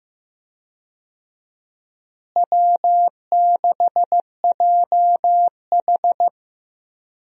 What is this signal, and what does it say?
Morse code, heard as a keyed tone, about 16 wpm, 700 Hz. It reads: W6JH
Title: USA Call Signs - 4 Characters - Repeated 3x - 15wpm